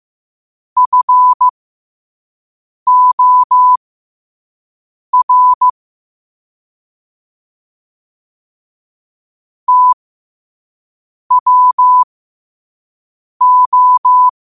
Morse code, heard as a keyed tone, about 15 wpm, spaced as for 5 wpm, 1000 Hz. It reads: FOR TWO
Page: New quotes every day in morse code at 5 Words per minute.